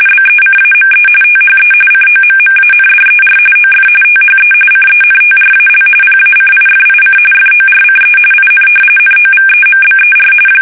RTTY, about 45.45 baud. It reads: BECAME THE COMMANDER OF THE CENTRAL PACIFIC FORCE IN NOVEMBER
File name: fsk850 / battleshipnj